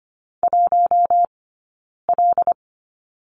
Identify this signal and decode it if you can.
Morse code, heard as a keyed tone, about 25 wpm, 700 Hz. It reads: 1 L
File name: course / 25:10wpm / 092 Lesson 16d - Letter c - Two Characters 25wpm